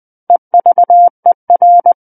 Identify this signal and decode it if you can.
Morse code, heard as a keyed tone, about 20 wpm, 700 Hz. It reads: EVER
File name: 11-Chapter 11 - 20 WPM